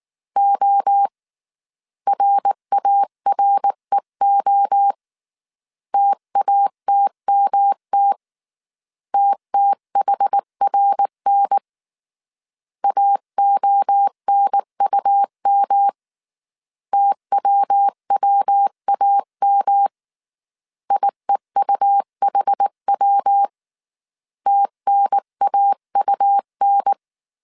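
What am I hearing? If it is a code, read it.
Morse code, heard as a keyed tone, about 19 wpm, 800 Hz. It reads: O RAREO TATMT TTHRN AONUM TWWAM IEUHW TNAUN
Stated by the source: Morse code dictation at 18 wpm from the IW3SGT QRS Morse code training course, five-character groups of 300 characters.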